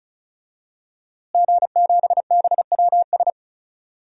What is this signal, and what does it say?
Morse code, heard as a keyed tone, about 34 wpm, 700 Hz. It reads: G7BWS
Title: DXCC-100 Call Signs - 5 Characters - Repeated 3x - 35wpm